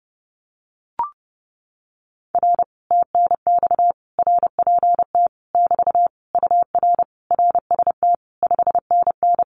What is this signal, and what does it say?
Morse code, heard as a keyed tone, about 32 wpm, 700 Hz. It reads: R TNX RPT <BT> UR RST 5NN
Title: Common QSO Phrases Encore 30wpm